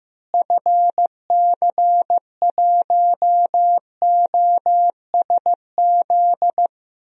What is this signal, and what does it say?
Morse code, heard as a keyed tone, about 15 wpm, 700 Hz. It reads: FC1OSZ
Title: DXCC-100 Call Signs - 6 Characters - Repeated 3x - 15wpm